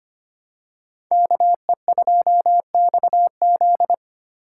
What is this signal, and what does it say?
Morse code, heard as a keyed tone, about 25 wpm, 700 Hz. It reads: KE2XZ
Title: USA Call Signs - 4x Character Spacing - 25wpm